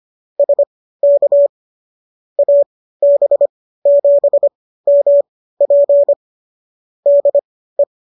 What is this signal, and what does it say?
Morse code, heard as a keyed tone, about 25 wpm, spaced as for 15 wpm, 570 Hz. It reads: SK AB7MP DE